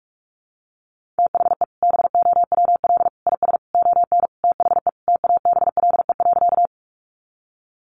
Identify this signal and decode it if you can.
Morse code, heard as a keyed tone, about 44 wpm, 700 Hz. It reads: THE BOWL IS ON THE TABLE.
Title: Sentences from Top 1200 Words 45wpm